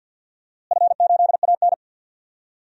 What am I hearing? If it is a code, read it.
Morse code, heard as a keyed tone, about 52 wpm, 700 Hz. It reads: F8AN
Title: DXCC-100 Call Signs - 4 Characters - Repeated 2x - 50wpm